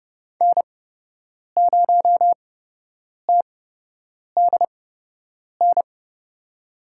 Morse code, heard as a keyed tone, about 30 wpm, spaced as for 8 wpm, 700 Hz. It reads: N0TDN
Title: USA Call Signs - 8x Character Spacing - 30wpm